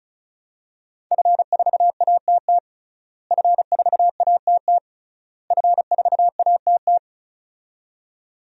Morse code, heard as a keyed tone, about 36 wpm, 700 Hz. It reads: F4ATT F4ATT F4ATT
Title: DXCC-100 Call Signs - 5 Characters - Repeated 3x - 35wpm